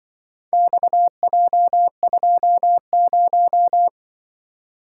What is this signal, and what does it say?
Morse code, heard as a keyed tone, about 24 wpm, 700 Hz. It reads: XJ20
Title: Speed-Racing - Four Letter-Number Combinations - 20wpm